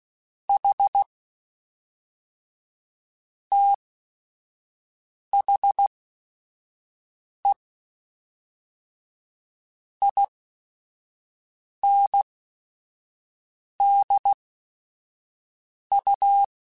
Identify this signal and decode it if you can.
Morse code, heard as a keyed tone, about 16 wpm, spaced as for 5 wpm, 800 Hz. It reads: H THE INDU
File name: Technical Subjects / Technical Subjects-mp3s / Reactance @5